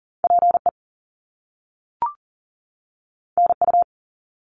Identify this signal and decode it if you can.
Morse code, heard as a keyed tone, about 40 wpm, 700 Hz. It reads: PE NU
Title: Canadian Provinces and Territories 40wpm